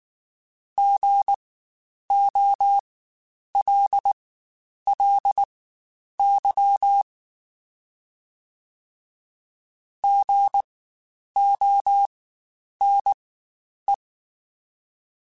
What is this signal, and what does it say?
Morse code, heard as a keyed tone, about 19 wpm, spaced as for 9 wpm, 800 Hz. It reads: GOLLY GONE
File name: Spaced Alphas / Spaced Alphas mp3s / Gs sp @18